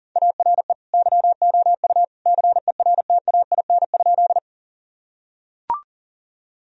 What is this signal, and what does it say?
Morse code, heard as a keyed tone, about 40 wpm, 700 Hz. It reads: ARE YOU CERTAIN?
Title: Sentences from Top 1000 Words - Review Part 2 - 40wpm